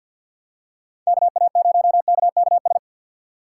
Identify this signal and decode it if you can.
Morse code, heard as a keyed tone, about 51 wpm, 700 Hz. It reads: KA0KKS
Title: USA Call Signs - 6 Characters - 4x Character Spacing - 50wpm